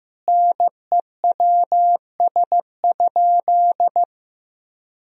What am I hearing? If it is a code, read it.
Morse code, heard as a keyed tone, about 15 wpm, 700 Hz. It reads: NEWS?
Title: Sentences from Top 1000 Words - Review Part 4 - 15wpm